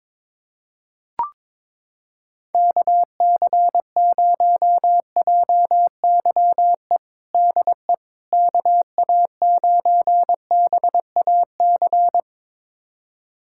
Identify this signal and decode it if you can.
Morse code, heard as a keyed tone, about 21 wpm, 700 Hz. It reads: KC0JYE DE KA9BAC
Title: USA Callsign DE Callsign - 6 Characters - 22wpm